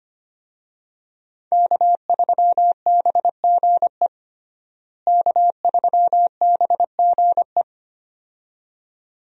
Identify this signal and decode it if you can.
Morse code, heard as a keyed tone, about 25 wpm, 700 Hz. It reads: K3BGE K3BGE
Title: USA Call Signs - 5 Characters - Repeated 2x - 25wpm